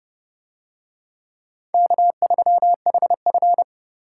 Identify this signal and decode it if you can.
Morse code, heard as a keyed tone, about 30 wpm, 700 Hz. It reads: K3HF